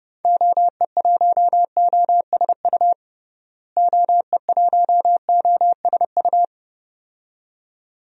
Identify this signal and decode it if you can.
Morse code, heard as a keyed tone, about 30 wpm, 700 Hz. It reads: OE1OSU OE1OSU
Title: DXCC-100 Call Signs - 6 Characters - Repeated 2x - 30wpm